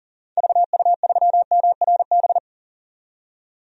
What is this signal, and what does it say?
Morse code, heard as a keyed tone, about 40 wpm, 700 Hz. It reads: VU3MRB
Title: DXCC-100 Call Signs - 6 Characters - 40wpm